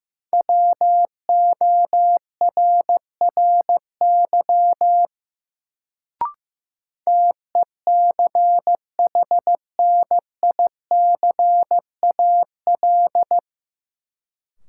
Morse code, heard as a keyed tone, about 15 wpm, 700 Hz. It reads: WORRY TECHNICAL
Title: Sentences from Top 1100 Words - New Word Drill - 15wpm